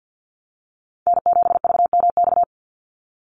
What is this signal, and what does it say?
Morse code, heard as a keyed tone, about 50 wpm, 700 Hz. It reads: N7VMX
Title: USA Call Signs 50wpm